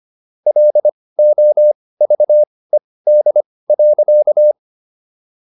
Morse code, heard as a keyed tone, about 25 wpm, spaced as for 18 wpm, 600 Hz. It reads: LOVED.